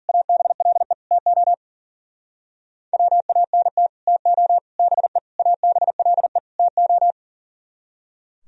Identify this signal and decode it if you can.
Morse code, heard as a keyed tone, about 51 wpm, 700 Hz. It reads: ABLE TO WANT TO BE ABLE TO